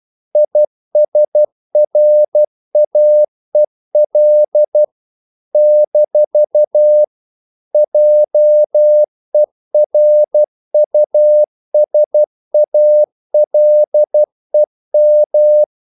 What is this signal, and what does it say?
Morse code, heard as a keyed tone, about 12 wpm, 600 Hz. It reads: ISRAEL - JERUSALEM
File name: CW / Practice / NationalCapitals12